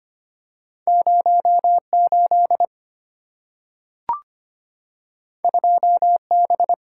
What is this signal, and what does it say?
Morse code, heard as a keyed tone, about 25 wpm, 700 Hz. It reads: 08 2B
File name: Two Letter-Number Combinations - 25wpm